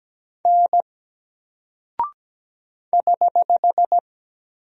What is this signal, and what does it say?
Morse code, heard as a keyed tone, about 17 wpm, 700 Hz. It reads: N <HH>
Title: Single Character (Letter:Number:Punctuation:Prosign) 17wpm